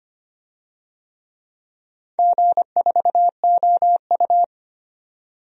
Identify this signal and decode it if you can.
Morse code, heard as a keyed tone, about 25 wpm, 700 Hz. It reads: G4OU
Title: DXCC-100 Call Signs - 4 Characters - 25wpm